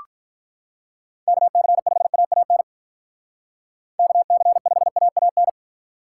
Morse code, heard as a keyed tone, about 47 wpm, 700 Hz. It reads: KC5AAN KC5AAN